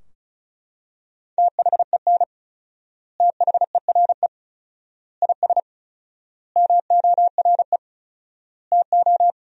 Morse code, heard as a keyed tone, about 36 wpm, 700 Hz. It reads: THEN THERE IS MORE TO
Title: 4x Word-Spacing - Sentences from Top 100 Words - 35wpm